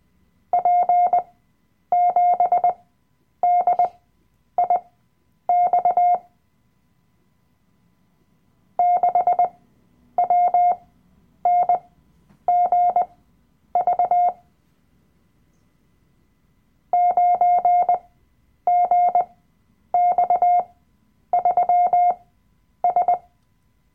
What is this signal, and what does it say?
Morse code, heard as a keyed tone, about 20 wpm, spaced as for 9 wpm, 710 Hz. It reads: P7DIX 6WNGV 9GX3S
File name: Workshop Dateien / CW-Online-Kurs / Lektion 007